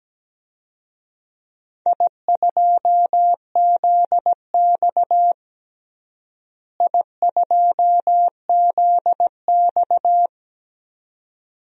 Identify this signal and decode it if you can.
Morse code, heard as a keyed tone, about 17 wpm, 700 Hz. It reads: I2ZX I2ZX
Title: DXCC-100 Call Signs - 4 Characters - Repeated 2x - 17wpm